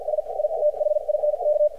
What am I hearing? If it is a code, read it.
Morse code, heard as a keyed tone, about 50 wpm, spaced as for 44 wpm, 600 Hz, with radio noise. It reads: S I THINK
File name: archived / CRSnet